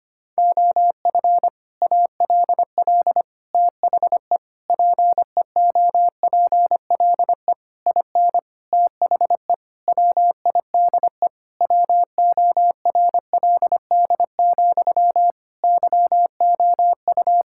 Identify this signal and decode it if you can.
Morse code, heard as a keyed tone, about 25 wpm, 700 Hz. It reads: OF ALL THE PEOPLE IN THE WIDE WORLD, YOU
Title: Sentences from Top 1000 Words - Review Part 6 - 25wpm